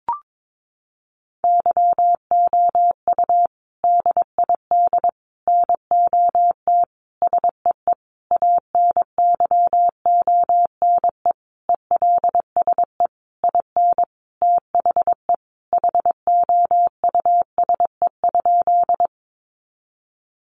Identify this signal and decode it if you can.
Morse code, heard as a keyed tone, about 22 wpm, 700 Hz. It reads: YOU DID NOT SEE ANYONE ELSE IN THE HOUSE?
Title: Sentences from Top 900 Words 22wpm